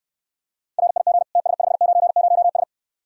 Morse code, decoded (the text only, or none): RIG TS590S